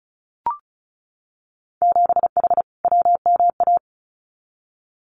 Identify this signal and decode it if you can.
Morse code, heard as a keyed tone, about 34 wpm, 700 Hz. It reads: 7H WMA
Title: Winter Field Day Exchanges Encore 35wpm